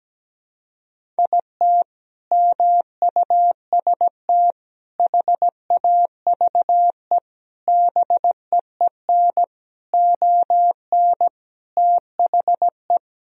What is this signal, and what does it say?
Morse code, heard as a keyed tone, about 17 wpm, 700 Hz. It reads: IT MUST HAVE BEEN ON THE
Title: Sentences from Top 1000 Words - Review Part 1 - 17wpm